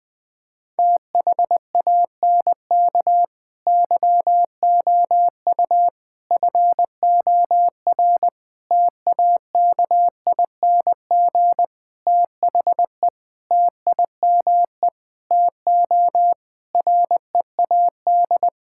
Morse code, heard as a keyed tone, about 20 wpm, 700 Hz. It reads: THANK YOU FOR TAKING THE TIME TO READ
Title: Sentences with 9-Word Phrases - Spoken Beginning - 20wpm